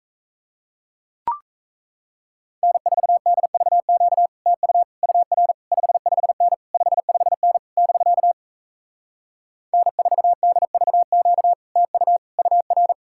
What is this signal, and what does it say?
Morse code, heard as a keyed tone, about 38 wpm, 700 Hz. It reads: N4DVQ TU UR 55N 55N <BK> N4DVQ TU UR